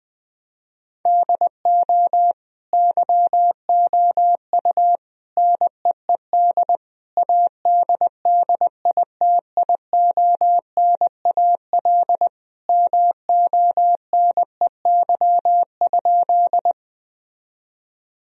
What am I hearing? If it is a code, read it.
Morse code, heard as a keyed tone, about 21 wpm, 700 Hz. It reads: DO YOU NEED ADDITIONAL MONEY?
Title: Sentences from Top 1000 Words - Review Part 4 - 20wpm